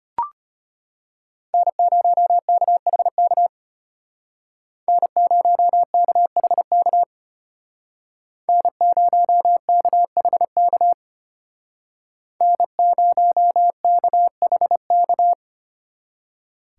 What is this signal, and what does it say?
Morse code, heard as a keyed tone, about 32 wpm, 700 Hz. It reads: N0KHK N0KHK N0KHK N0KHK